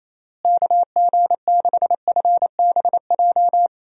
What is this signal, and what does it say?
Morse code, heard as a keyed tone, about 28 wpm, 700 Hz. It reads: KG6FBJ
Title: USA Call Signs - 6 Characters - Repeated 2x - 28wpm